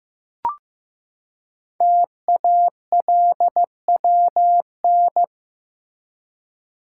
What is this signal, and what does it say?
Morse code, heard as a keyed tone, about 15 wpm, 700 Hz. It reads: TALWN
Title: Code Groups - 5 Letters - 15wpm